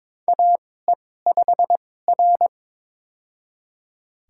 Morse code, heard as a keyed tone, about 23 wpm, spaced as for 16 wpm, 700 Hz. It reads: AE5R
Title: USA Call Signs - 4 Characters - 2x Character Spacing - 22wpm